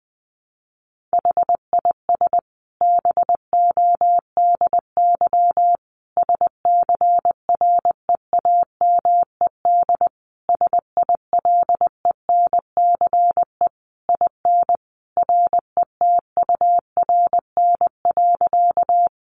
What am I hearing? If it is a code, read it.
Morse code, heard as a keyed tone, about 20 wpm, 700 Hz. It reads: HIS BODY SCREAMED SILENCE IN RETURN.
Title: A Sound Of Thunder 20wpm